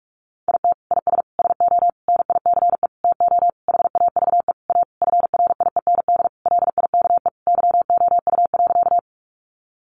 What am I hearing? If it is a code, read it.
Morse code, heard as a keyed tone, about 44 wpm, 700 Hz. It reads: IT IS SO NICE TO HAVE A FRIEND LIKE YOU.
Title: Sentences from Top 1100 Words 45wpm